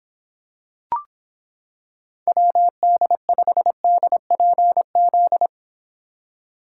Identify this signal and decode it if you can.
Morse code, heard as a keyed tone, about 26 wpm, 700 Hz. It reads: WD5DPZ